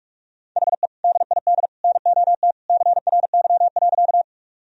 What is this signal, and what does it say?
Morse code, heard as a keyed tone, about 42 wpm, 700 Hz. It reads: HE DID NOT CRY.
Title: Sentences from Top 1000 Words - Review Part 2 - 45wpm